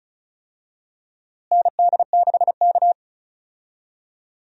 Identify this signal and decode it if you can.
Morse code, heard as a keyed tone, about 36 wpm, 700 Hz. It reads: ND6K
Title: USA Call Signs - 4 Characters - Repeated 3x - 35wpm